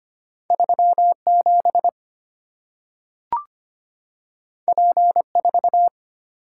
Morse code, heard as a keyed tone, about 25 wpm, 700 Hz. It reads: 37 P4
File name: Two Letter-Number Combinations - 25wpm